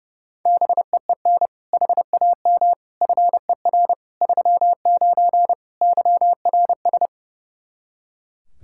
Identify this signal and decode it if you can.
Morse code, heard as a keyed tone, about 32 wpm, 700 Hz. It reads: BEEN HAM FER 39 YRS
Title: Practice QSO - Exchange 3 - More Info - 30wpm